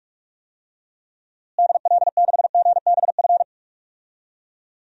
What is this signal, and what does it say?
Morse code, heard as a keyed tone, about 46 wpm, 700 Hz. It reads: DL6GBF
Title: DXCC-100 Call Signs - 6 Characters - Repeated 2x - 45wpm